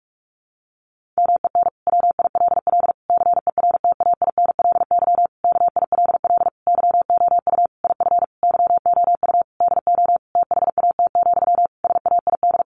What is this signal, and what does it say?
Morse code, heard as a keyed tone, about 44 wpm, 700 Hz. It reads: MEN WILL CERTAINLY KILL YOU IF YOU DO THAT, SAID
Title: Sentences from Top 1500 Words 45wpm